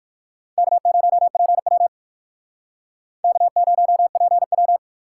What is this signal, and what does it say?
Morse code, heard as a keyed tone, about 59 wpm, 700 Hz. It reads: K0PW K0PW